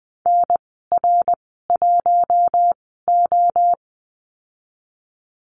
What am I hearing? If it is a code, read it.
Morse code, heard as a keyed tone, about 20 wpm, spaced as for 14 wpm, 700 Hz. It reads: NR1O
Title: 154 Lesson 25g - Letter p - Call Signs 20wpm